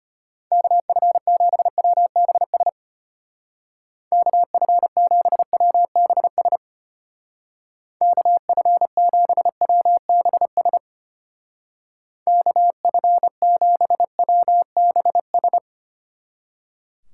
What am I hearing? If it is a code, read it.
Morse code, heard as a keyed tone, about 32 wpm, 700 Hz. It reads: KF7WBS KF7WBS KF7WBS KF7WBS